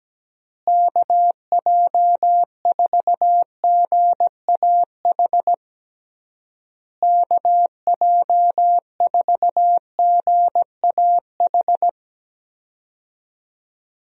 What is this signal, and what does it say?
Morse code, heard as a keyed tone, about 17 wpm, 700 Hz. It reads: KJ4GAH KJ4GAH
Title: USA Call Signs - 6 Characters - Repeated 3x - 17wpm